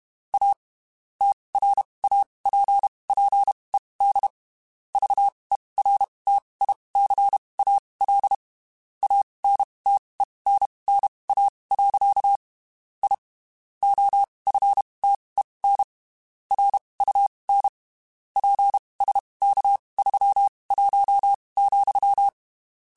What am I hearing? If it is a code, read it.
Morse code, heard as a keyed tone, about 31 wpm, spaced as for 23 wpm, 800 Hz. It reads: A TRAPPED VERTICAL ANTENNA. I OFTEN RUN PSK31,
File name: (spaced) / QSOs Spaced-mp3s / kc7sac @30 sp